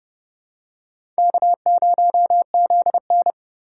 Morse code, heard as a keyed tone, about 27 wpm, 700 Hz. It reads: K0ZN